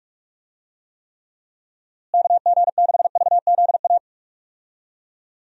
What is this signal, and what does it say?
Morse code, heard as a keyed tone, about 42 wpm, 700 Hz. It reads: KG6VZA
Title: USA Call Signs - 6 Characters - Repeated 3x - 45wpm